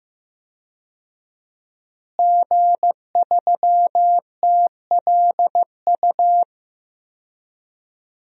Morse code, heard as a keyed tone, about 15 wpm, 700 Hz. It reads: G3TLU
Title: DXCC-100 Call Signs - 5 Characters - Repeated 3x - 15wpm